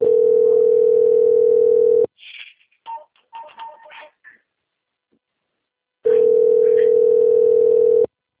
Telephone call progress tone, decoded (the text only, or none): ringback tone